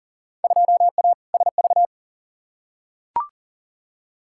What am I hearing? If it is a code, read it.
Morse code, heard as a keyed tone, about 39 wpm, 700 Hz. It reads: 2A SV
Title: Field Day Exchanges - 40wpm